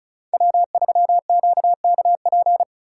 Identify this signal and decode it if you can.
Morse code, heard as a keyed tone, about 36 wpm, 700 Hz. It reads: W3QKP
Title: USA Call Signs - 3x Character Spacing - 35wpm